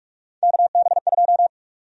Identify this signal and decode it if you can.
Morse code, heard as a keyed tone, about 46 wpm, 700 Hz. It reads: KB2
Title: USA Call Sign Prefixes - 45wpm